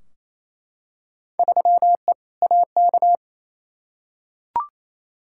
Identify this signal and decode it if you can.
Morse code, heard as a keyed tone, about 28 wpm, 700 Hz. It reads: 3E AK
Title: Field Day Exchanges Encore - 28wpm